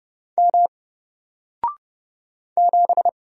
Morse code, heard as a keyed tone, about 30 wpm, 700 Hz. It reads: M 7
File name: Single-Letter-Number-30wpm